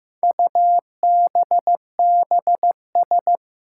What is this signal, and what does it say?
Morse code, heard as a keyed tone, about 15 wpm, 700 Hz. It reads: UBBS
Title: Four Challenging Letter-Number Combinations 15wpm